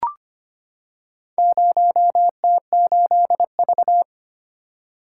Morse code, heard as a keyed tone, about 25 wpm, 700 Hz. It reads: 0T8V
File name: practice / 25wpm / Four Letter-Number Combinations - 25wpm